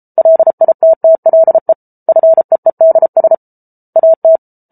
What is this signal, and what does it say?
Morse code, heard as a keyed tone, about 34 wpm, 650 Hz. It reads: LITTLE FEEDS AT